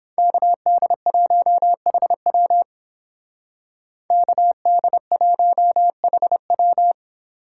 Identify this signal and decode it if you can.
Morse code, heard as a keyed tone, about 31 wpm, 700 Hz. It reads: KD1HW KD1HW